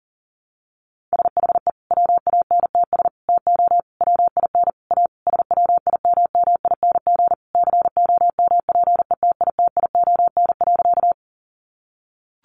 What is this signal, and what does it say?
Morse code, heard as a keyed tone, about 40 wpm, 700 Hz. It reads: SHE WANTS TO WIN A SWIMMING COMPETITION.
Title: Additional Easy Phrases 40wpm